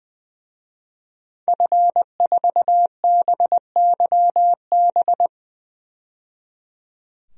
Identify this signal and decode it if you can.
Morse code, heard as a keyed tone, about 20 wpm, 700 Hz. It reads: F4BYB
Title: DXCC-100 Call Signs - 5 Characters - 20wpm